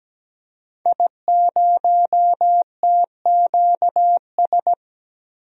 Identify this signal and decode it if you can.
Morse code, heard as a keyed tone, about 17 wpm, 700 Hz. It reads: I0TQS
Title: DXCC-100 Call Signs - 5 Characters - 17wpm